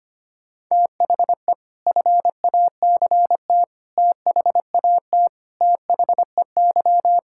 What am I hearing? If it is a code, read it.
Morse code, heard as a keyed tone, about 25 wpm, 700 Hz. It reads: THE FACT THAT THEY
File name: practice / 25wpm / Sentences with 4-Word Phrases - 25wpm